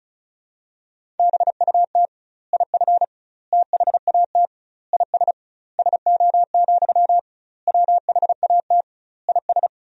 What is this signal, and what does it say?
Morse code, heard as a keyed tone, about 36 wpm, 700 Hz. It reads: BUT IF THAT IS SO, WHAT IS
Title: 2x Word-Spacing - Sentences from Top 200 Words - 35wpm